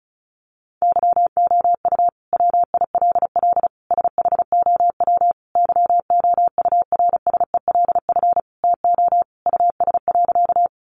Morse code, heard as a keyed tone, about 34 wpm, 700 Hz. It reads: YOU WILL SHOW YOURSELF TO US.
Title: Sentences from Top 600 Words 35wpm